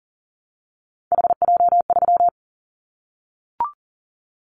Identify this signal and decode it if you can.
Morse code, heard as a keyed tone, about 40 wpm, 700 Hz. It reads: HJ3
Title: Callsign Prefixes for Top 100 DXCC - 40wpm